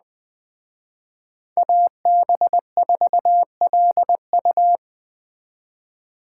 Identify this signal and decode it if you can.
Morse code, heard as a keyed tone, about 20 wpm, 700 Hz. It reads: AB4LU